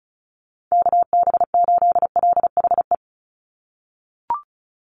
Morse code, heard as a keyed tone, about 34 wpm, 700 Hz. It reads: KB8LHE
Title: USA Call Signs 35wpm